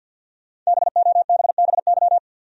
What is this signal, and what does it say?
Morse code, heard as a keyed tone, about 46 wpm, 700 Hz. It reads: BOBBY
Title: USA Common Names - 50wpm